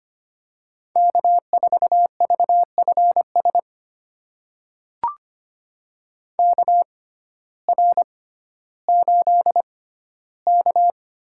Morse code, heard as a keyed tone, about 25 wpm, spaced as for 14 wpm, 700 Hz. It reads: K4VFS KR8K
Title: USA Call Signs - 6x Character Spacing - 25wpm